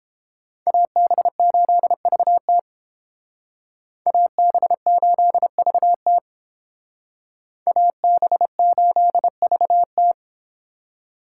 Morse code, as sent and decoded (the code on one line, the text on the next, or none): .- -... ---.. ...- - / .- -... ---.. ...- - / .- -... ---.. ...- -
AB8VT AB8VT AB8VT